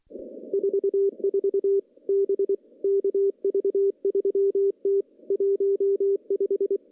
Morse code, heard as a keyed tone, about 23 wpm, 400 Hz. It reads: 44 B KV3T 15
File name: CW-SS-k5tr